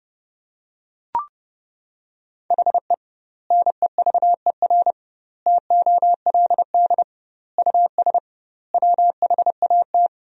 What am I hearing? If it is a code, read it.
Morse code, heard as a keyed tone, about 30 wpm, 700 Hz. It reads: HE NEVER TOLD US WHAT
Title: 2x Word-Spacing - Sentences from Top 400 Words - 30wpm